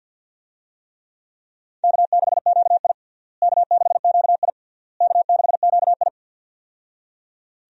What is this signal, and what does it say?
Morse code, heard as a keyed tone, about 52 wpm, 700 Hz. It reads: K6QI K6QI K6QI
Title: USA Call Signs - 4 Characters - Repeated 3x - 50wpm